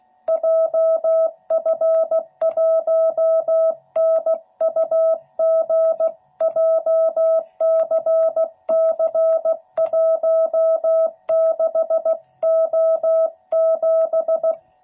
Morse code, heard as a keyed tone, about 16 wpm, 660 Hz. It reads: JF1NUG JCC16O7
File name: jh8cbh / 2021cw / cw12b